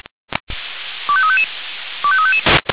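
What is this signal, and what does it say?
Five-tone selective call (ZVEI), heard as quick runs of five tones, tones of about 70 ms. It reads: 25355 25355
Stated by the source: ZVEI five-tone selective call received on an SDR.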